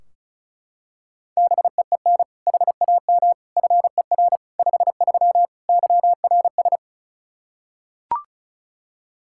Morse code, heard as a keyed tone, about 36 wpm, 700 Hz. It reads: BEEN HAM FER 53 YRS
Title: Practice QSO - Exchange 3 - More Info - 35wpm